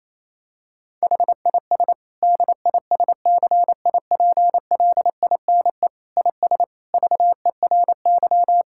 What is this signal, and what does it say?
Morse code, heard as a keyed tone, about 28 wpm, 700 Hz. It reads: HIS DISCIPLINE IS VERY
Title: Sentences from Top 1000 Words - Review Part 6 - 28wpm